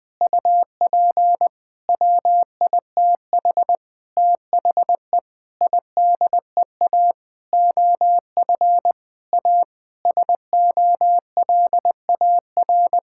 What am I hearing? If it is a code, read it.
Morse code, heard as a keyed tone, about 20 wpm, 700 Hz. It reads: UP WITH THE IDEA OF A SOLAR
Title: Sentences with 6-Word Phrases - 20wpm